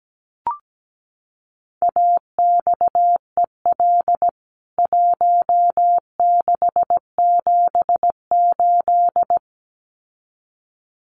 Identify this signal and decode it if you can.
Morse code, heard as a keyed tone, about 17 wpm, 700 Hz. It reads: AXEL 1678
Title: CWT Exchange 17wpm